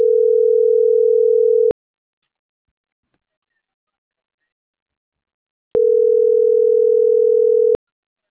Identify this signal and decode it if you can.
Ringback tone (North America), 440 + 480 Hz.